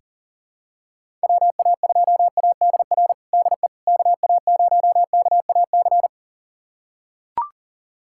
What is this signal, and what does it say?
Morse code, heard as a keyed tone, about 40 wpm, 700 Hz. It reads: WA2ADR DE KA0KAC
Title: USA Callsign DE Callsign - 6 Characters - 40wpm